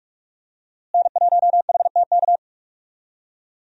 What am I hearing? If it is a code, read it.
Morse code, heard as a keyed tone, about 46 wpm, 700 Hz. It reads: N1HTK
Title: USA Call Signs - 5 Characters - Repeated 3x - 45wpm